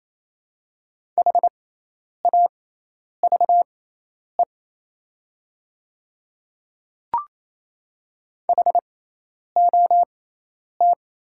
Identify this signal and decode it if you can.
Morse code, heard as a keyed tone, about 28 wpm, spaced as for 10 wpm, 690 Hz. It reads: HAVE HOT
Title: 6x Character Spacing - Top 100 Words - 28wpm